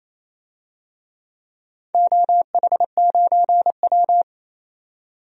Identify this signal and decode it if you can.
Morse code, heard as a keyed tone, about 28 wpm, 700 Hz. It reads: OH9W
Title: DXCC-100 Call Signs - 4 Characters - 28wpm